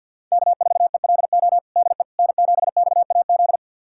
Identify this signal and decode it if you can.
Morse code, heard as a keyed tone, about 50 wpm, 700 Hz. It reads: K4ELO DE N7XAZ
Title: USA Callsign DE Callsign 50wpm